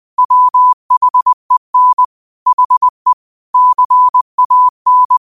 Morse code, heard as a keyed tone, about 20 wpm, 1000 Hz. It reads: WHEN HE CAN